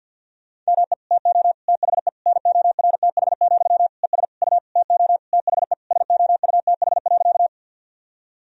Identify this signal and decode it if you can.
Morse code, heard as a keyed tone, about 51 wpm, 700 Hz. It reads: ME TO THE NORTH, ES U TO THE SOUTH.
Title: Sentences from Top 400 Words and Abbreviations 50wpm